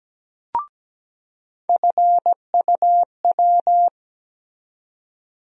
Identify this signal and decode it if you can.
Morse code, heard as a keyed tone, about 17 wpm, 700 Hz. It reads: FUW
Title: USA Call Sign Postfixes - 3 Letters 17wpm